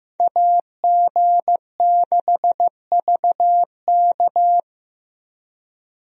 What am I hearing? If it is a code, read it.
Morse code, heard as a keyed tone, about 15 wpm, 700 Hz. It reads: AG6VK
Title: USA Call Signs - 5 Characters - Repeated 3x - 15wpm